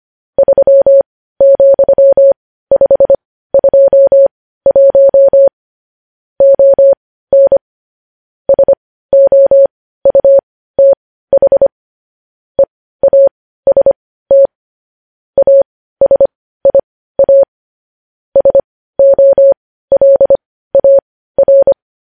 Morse code, heard as a keyed tone, about 25 wpm, spaced as for 15 wpm, 550 Hz. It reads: 3,521 ON SOUTH EAST ASIA SOLAR
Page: News headlines in morse code at 15 wpm